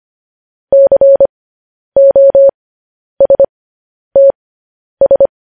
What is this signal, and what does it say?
Morse code, heard as a keyed tone, about 25 wpm, spaced as for 10 wpm, 550 Hz. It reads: COSTS
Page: News Headlines in Morse Code at 10 WPM